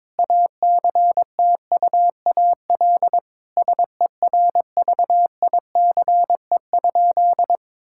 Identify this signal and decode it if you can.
Morse code, heard as a keyed tone, about 22 wpm, 700 Hz. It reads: ACTUAL SERVICE?
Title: Sentences from Top 1000 Words - Review Part 5 - 22wpm